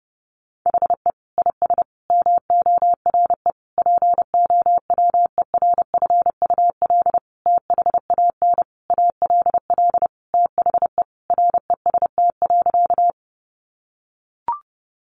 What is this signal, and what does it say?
Morse code, heard as a keyed tone, about 30 wpm, 700 Hz. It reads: HE IS MORE POWERFUL THAN ALL THE REST.
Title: Sentences from Top 1300 Words 30wpm